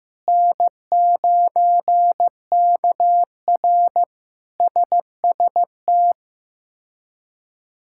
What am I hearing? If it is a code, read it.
Morse code, heard as a keyed tone, about 15 wpm, 700 Hz. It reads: N9KR SST
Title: K1USN SST QSO Practice - 15wpm